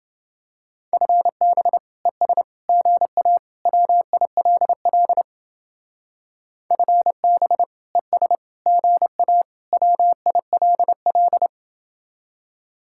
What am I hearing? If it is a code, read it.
Morse code, heard as a keyed tone, about 29 wpm, 700 Hz. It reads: FB ES GA WILL FB ES GA WILL